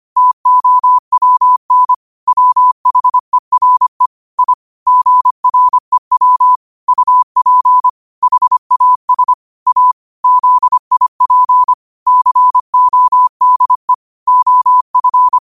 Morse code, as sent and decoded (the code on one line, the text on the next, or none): - --- .-- -. / .-- .... .-. / .. / --. .-. .-- / ..- .--. / .... .- ... / .- / --.. .. .--. / -.-. --- -.. / --- ..-.
TOWN WHERE I GREW UP HAS A ZIP CODE OF